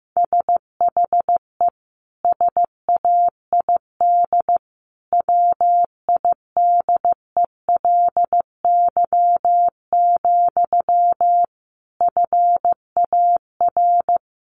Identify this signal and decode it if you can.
Morse code, heard as a keyed tone, about 14 wpm, 700 Hz. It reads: SHE SAID WIDELY, FAR
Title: Sentences from Top 1200 Words 15wpm